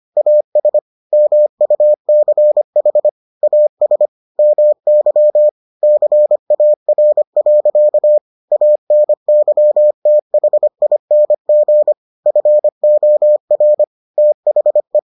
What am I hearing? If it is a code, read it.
Morse code, heard as a keyed tone, about 25 wpm, 600 Hz. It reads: AS MUCH AS MY CAR. ANYTHING FOR THE